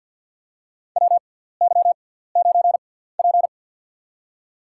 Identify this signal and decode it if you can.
Morse code, heard as a keyed tone, about 47 wpm, spaced as for 17 wpm, 700 Hz. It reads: WY9P
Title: USA Call Signs - 4 Characters - 6x Character Spacing - 50wpm